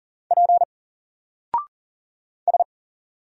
Morse code, heard as a keyed tone, about 39 wpm, 700 Hz. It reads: P S
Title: Single Character (Letter:Number:Punctuation:Prosign) 40wpm